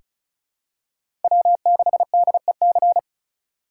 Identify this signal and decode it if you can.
Morse code, heard as a keyed tone, about 36 wpm, 700 Hz. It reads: W6DEC
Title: USA Call Signs - 5 Characters - 4x Character Spacing - 35wpm